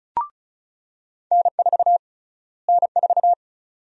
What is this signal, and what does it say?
Morse code, heard as a keyed tone, about 36 wpm, 700 Hz. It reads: N4 N4
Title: USA Call Sign Prefixes - Repeated 2x - 35wpm